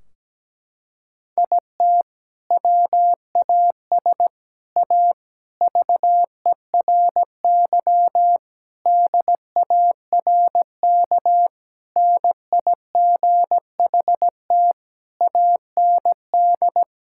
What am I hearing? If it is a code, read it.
Morse code, heard as a keyed tone, about 17 wpm, 700 Hz. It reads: IT WAS A VERY DARK NIGHT AND
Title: Sentences from Top 1600 Words 17wpm